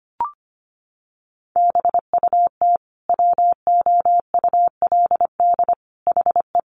Morse code, heard as a keyed tone, about 25 wpm, 700 Hz. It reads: BUT WOULD HE
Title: Sentences from Top 1400 Words 25wpm